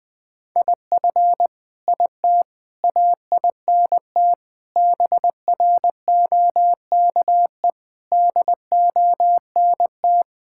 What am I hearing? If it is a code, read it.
Morse code, heard as a keyed tone, about 20 wpm, 700 Hz. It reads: IF IT AINT BROKE DONT
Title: Sentences with 7-Word Phrases - 20wpm